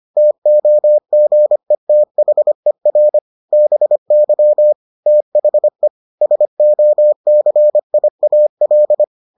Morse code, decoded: TOGETHER BY THE SOCIAL

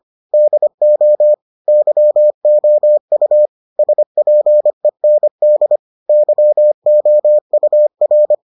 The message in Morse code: DO YOU SPEND YOUR

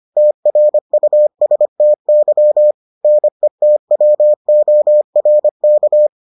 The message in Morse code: TRUSTY NETWORK

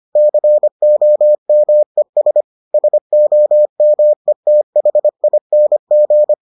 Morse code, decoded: COMES SOMETHING

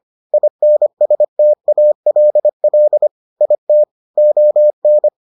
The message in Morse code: INSTALL IT ON

